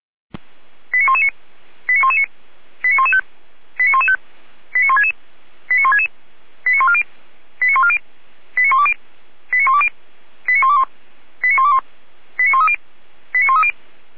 Five-tone selective call (ZVEI), heard as quick runs of five tones, tones of about 70 ms. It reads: 89119 89119 89115 89115 89166 89166 89140 89140 89120 89120 89121 89121 89130 89130